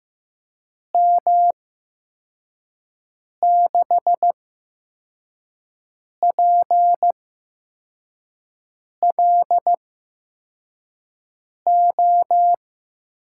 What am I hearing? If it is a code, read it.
Morse code, heard as a keyed tone, about 15 wpm, spaced as for 4 wpm, 700 Hz. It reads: M6PLO
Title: USA Call Signs - 6 Characters - 8x Character Spacing - 15wpm